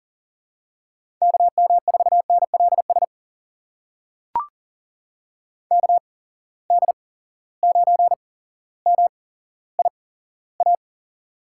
Morse code, heard as a keyed tone, about 37 wpm, spaced as for 17 wpm, 700 Hz. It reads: KM4NLS KD9MIA
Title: USA Call Signs - 6 Characters - 8x Character Spacing - 40wpm